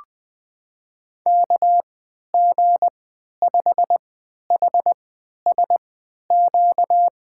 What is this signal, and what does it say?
Morse code, heard as a keyed tone, about 20 wpm, spaced as for 11 wpm, 700 Hz. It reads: KG5HSQ
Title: USA Call Signs - 6 Characters - 3x Character Spacing - 20wpm